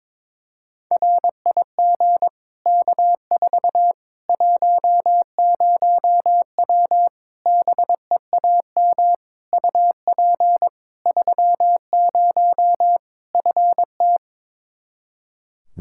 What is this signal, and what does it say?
Morse code, heard as a keyed tone, about 22 wpm, 700 Hz. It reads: RIG K4 10W BEAM UP 30 FT
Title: K1USN SST QSO Practice 22wpm